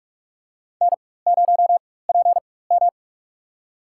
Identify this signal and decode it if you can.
Morse code, heard as a keyed tone, about 46 wpm, spaced as for 21 wpm, 700 Hz. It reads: N0PM
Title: USA Call Signs - 4x Character Spacing - 45wpm